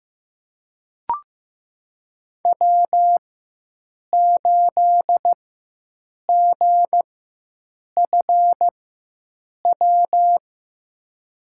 Morse code, heard as a keyed tone, about 16 wpm, spaced as for 7 wpm, 700 Hz. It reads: W8GFW
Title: USA Call Signs - 5 Characters - 4x Character Spacing - 15wpm